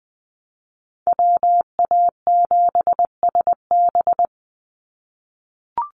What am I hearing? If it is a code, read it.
Morse code, heard as a keyed tone, about 20 wpm, 690 Hz. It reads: WA7SB
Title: NA SOTA Chaser Callsigns 20wpm